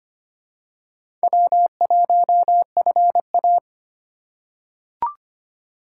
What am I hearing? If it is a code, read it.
Morse code, heard as a keyed tone, about 25 wpm, 700 Hz. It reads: W1FA
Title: USA Call Signs - 2x Character Spacing - 25wpm